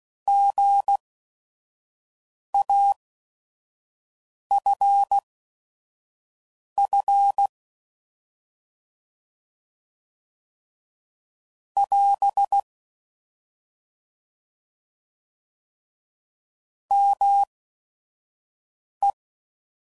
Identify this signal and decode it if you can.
Morse code, heard as a keyed tone, about 16 wpm, spaced as for 5 wpm, 800 Hz. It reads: GAFF <AS> ME